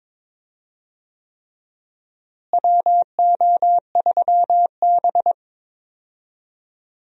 Morse code, heard as a keyed tone, about 22 wpm, 700 Hz. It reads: WO3B
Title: USA Call Signs - 4 Characters - Repeated 3x - 22wpm